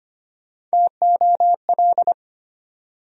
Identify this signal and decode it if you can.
Morse code, heard as a keyed tone, about 25 wpm, 700 Hz. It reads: TOL
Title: USA Call Sign Postfixes - 3 Letters 25wpm